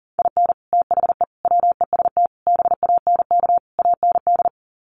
Morse code, heard as a keyed tone, about 40 wpm, 700 Hz. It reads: IN THE WEST BANK AND
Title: Sets of 5 Words Encore2 40wpm